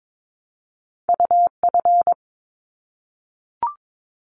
Morse code, heard as a keyed tone, about 22 wpm, 700 Hz. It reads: UF
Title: Top Two Letter Combinations in Words 22wpm